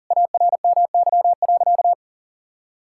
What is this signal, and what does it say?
Morse code, heard as a keyed tone, about 40 wpm, 700 Hz. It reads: ARMY.
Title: Sentences from Top 1000 Words - Review Part 6 - 40wpm